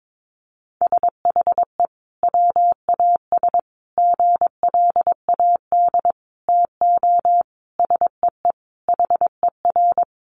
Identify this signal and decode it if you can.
Morse code, heard as a keyed tone, about 22 wpm, 700 Hz. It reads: SHE WAS GLAD TO SEE HER
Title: Sentences from Top 800 Words 22wpm